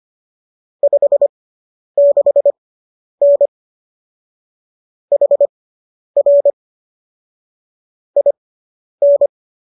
Morse code, heard as a keyed tone, about 25 wpm, spaced as for 10 wpm, 570 Hz. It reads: 56N HR IN